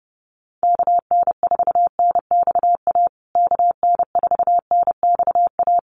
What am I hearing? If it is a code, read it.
Morse code, heard as a keyed tone, about 30 wpm, 700 Hz. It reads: KN4NXA KN4NXA
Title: USA Call Signs Repeated 3x 30wpm